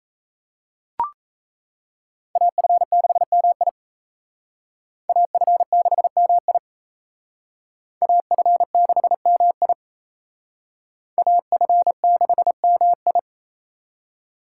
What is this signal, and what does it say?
Morse code, heard as a keyed tone, about 35 wpm, 700 Hz. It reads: AF6MI AF6MI AF6MI AF6MI
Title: Speed Racing - USA 2x2 Call Signs 28wpm